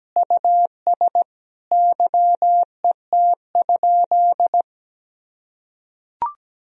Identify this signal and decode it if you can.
Morse code, heard as a keyed tone, about 17 wpm, 700 Hz. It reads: US YET?
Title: Sentences from Top 1000 Words - Review Part 3 - 17wpm